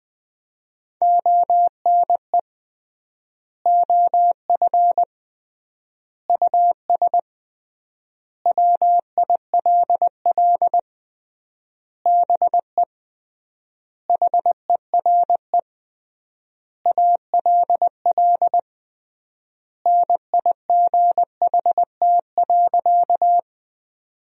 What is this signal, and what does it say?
Morse code, heard as a keyed tone, about 20 wpm, 700 Hz. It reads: ONE OF US WILL BE HERE ALL NIGHT.
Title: 3x Word-Spacing - Sentences from Top 300 Words - 20wpm